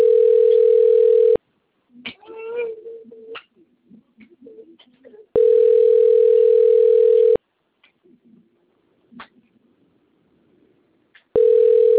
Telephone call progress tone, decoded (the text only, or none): ringback tone